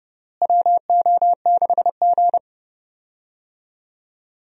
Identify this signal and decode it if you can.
Morse code, heard as a keyed tone, about 30 wpm, 700 Hz. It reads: WO6G
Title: USA Call Signs - 4 Characters - Repeated 3x - 30wpm